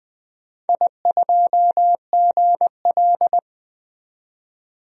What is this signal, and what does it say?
Morse code, heard as a keyed tone, about 20 wpm, 700 Hz. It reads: I2GL
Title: DXCC-100 Call Signs - 4 Characters - 20wpm